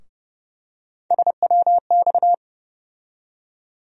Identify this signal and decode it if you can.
Morse code, heard as a keyed tone, about 32 wpm, 700 Hz. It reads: SWX
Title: USA Call Sign Postfixes - 3 Letters 30wpm